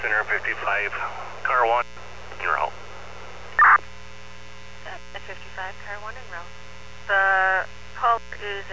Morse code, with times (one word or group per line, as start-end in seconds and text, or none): none